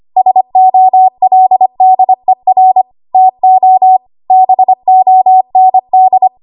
Morse code, heard as a keyed tone, about 25 wpm, 750 Hz. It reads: SOLDER TO BOND
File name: Archive / 25 WPM / 25WPM